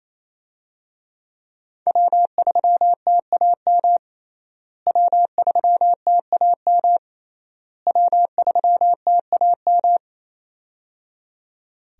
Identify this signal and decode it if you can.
Morse code, heard as a keyed tone, about 27 wpm, 700 Hz. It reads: W3TAM W3TAM W3TAM
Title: USA Call Signs - 5 Characters - Repeated 3x - 28wpm